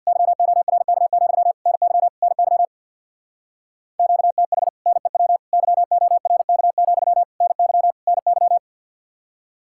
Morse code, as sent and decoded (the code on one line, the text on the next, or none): -.-- --- .-. -.- --..-- / -. -.-- / -. -.-- / --.- - .... / -. .-- / -.-- --- .-. -.- --..-- / -. -.-- / -. -.--
YORK, NY NY QTH NEW YORK, NY NY